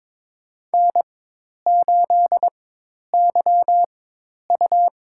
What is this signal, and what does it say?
Morse code, heard as a keyed tone, about 22 wpm, spaced as for 10 wpm, 700 Hz. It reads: N8YU
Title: USA Call Signs - 4x Character Spacing - 22wpm